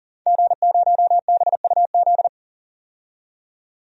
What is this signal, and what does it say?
Morse code, heard as a keyed tone, about 40 wpm, 700 Hz. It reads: G0BUZ
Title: DXCC-100 Call Signs - 5 Characters - 40wpm